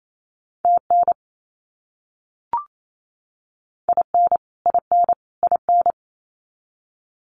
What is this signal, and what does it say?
Morse code, heard as a keyed tone, about 28 wpm, 700 Hz. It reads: TN IN IN IN
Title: US States Repeated 3x 28wpm